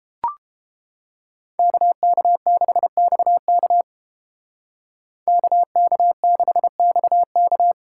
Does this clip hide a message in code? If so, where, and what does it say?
Yes: Morse code from 1.6 to 7.7 s, heard as a keyed tone, about 31 wpm, 700 Hz. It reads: KK6XK KK6XK